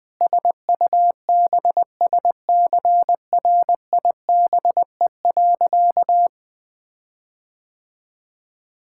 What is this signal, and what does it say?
Morse code, heard as a keyed tone, about 21 wpm, 700 Hz. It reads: SUBSCRIBE.
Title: Sentences with 8-Word Phrases - Spoken Beginning - 20wpm